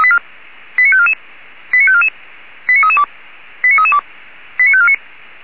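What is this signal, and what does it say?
Five-tone selective call (ZVEI), heard as quick runs of five tones, tones of about 70 ms. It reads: 89544 89544 89332 89332 89549
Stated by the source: ZVEI five-tone selective call received on SDR, from feuerwehr-orlamuende.org fire service audio server.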